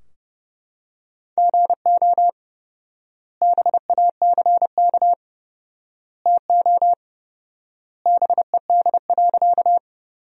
Morse code, heard as a keyed tone, about 30 wpm, 700 Hz. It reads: GO BACK TO BED.
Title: 4x Word-Spacing - Sentences from Top 500 Words - 30wpm